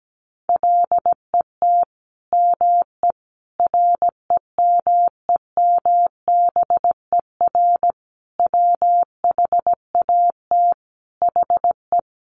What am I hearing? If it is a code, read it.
Morse code, heard as a keyed tone, about 17 wpm, 700 Hz. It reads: LET ME REMEMBER WHAT HE
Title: Sentences from Top 400 Words 17wpm